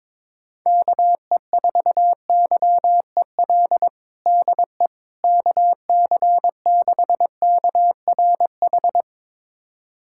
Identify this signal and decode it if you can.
Morse code, heard as a keyed tone, about 21 wpm, 700 Hz. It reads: KE4YEL DE KC6KRH
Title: USA Callsign DE Callsign 22wpm